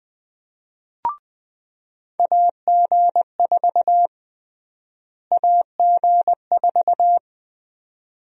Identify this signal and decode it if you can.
Morse code, heard as a keyed tone, about 20 wpm, 700 Hz. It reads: AG4 AG4
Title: USA Call Sign Prefixes - Repeated 2x - 20wpm